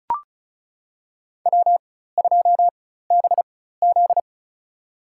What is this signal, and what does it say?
Morse code, heard as a keyed tone, about 34 wpm, spaced as for 16 wpm, 700 Hz. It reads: W2BZ
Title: USA Call Signs - 4 Characters - 4x Character Spacing - 35wpm